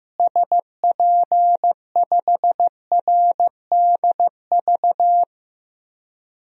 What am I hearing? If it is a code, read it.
Morse code, heard as a keyed tone, about 15 wpm, 700 Hz. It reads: SP5RDV
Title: DXCC-100 Call Signs - 6 Characters - Repeated 3x - 15wpm